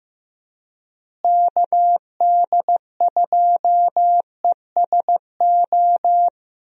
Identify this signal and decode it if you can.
Morse code, heard as a keyed tone, about 15 wpm, 700 Hz. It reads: KD2ESO
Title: USA Call Signs - 6 Characters - Repeated 3x - 15wpm